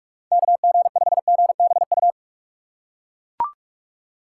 Morse code, heard as a keyed tone, about 41 wpm, 700 Hz. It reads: KG5GBU
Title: USA Call Signs - 3x Character Spacing - 45wpm